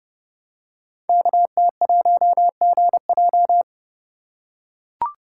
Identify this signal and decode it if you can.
Morse code, heard as a keyed tone, about 30 wpm, 700 Hz. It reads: KT1GJ
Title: Code Groups - 5 Letter:Numbers - 30wpm